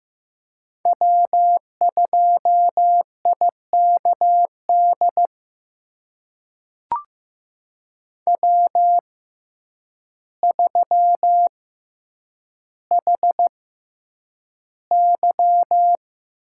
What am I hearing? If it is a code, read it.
Morse code, heard as a keyed tone, about 15 wpm, spaced as for 8 wpm, 700 Hz. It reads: W2IKD W3HY
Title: USA Call Signs - 6x Character Spacing - 15wpm